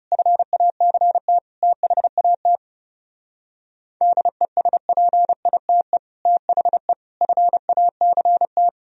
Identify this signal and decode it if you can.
Morse code, heard as a keyed tone, about 37 wpm, 700 Hz. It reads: FACT THAT DESPITE THE FACT
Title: Speed-Racing - Sets of 4 Words 30wpm